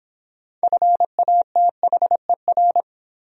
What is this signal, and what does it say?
Morse code, heard as a keyed tone, about 23 wpm, 700 Hz. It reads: FATHER